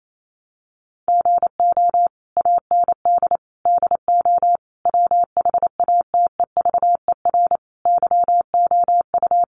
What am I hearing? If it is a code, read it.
Morse code, heard as a keyed tone, about 28 wpm, 700 Hz. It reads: GO AND DO WHATEVER YOU
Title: Sentences from Top 1100 Words 28wpm